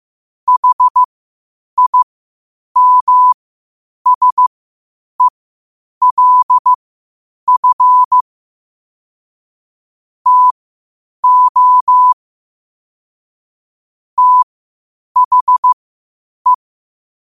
Morse code, heard as a keyed tone, about 15 wpm, spaced as for 8 wpm, 1000 Hz. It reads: HIMSELF TO THE
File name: qotdfiles / 8wpm